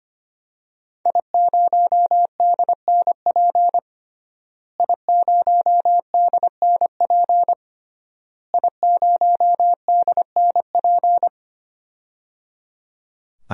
Morse code, heard as a keyed tone, about 25 wpm, 700 Hz. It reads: I0DNP I0DNP I0DNP
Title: DXCC-100 Call Signs - 5 Characters - Repeated 3x - 25wpm